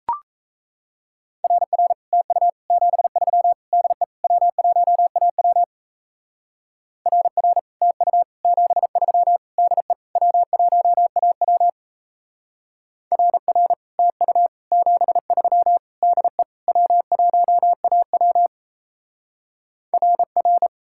Morse code, heard as a keyed tone, about 34 wpm, 700 Hz. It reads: RR TU 73 DE W1AW RR TU 73 DE W1AW RR TU 73 DE W1AW RR
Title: Speed-Racing Common QSO Phrases Encore - 28wpm